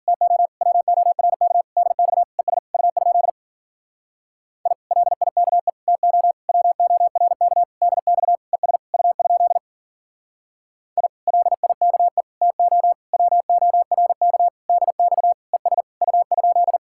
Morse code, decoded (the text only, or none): TO WORK DX ES U? I LIKE TO WORK DX ES U? I LIKE TO WORK DX ES U?